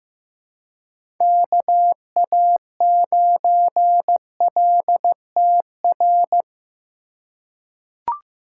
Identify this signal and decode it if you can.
Morse code, heard as a keyed tone, about 16 wpm, 700 Hz. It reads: KA9LTR
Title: USA Call Signs - 4x Character Spacing - 15wpm